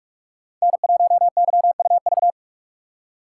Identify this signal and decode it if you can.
Morse code, heard as a keyed tone, about 45 wpm, 700 Hz. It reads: N1YUV
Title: USA Call Signs - 6x Character Spacing - 45wpm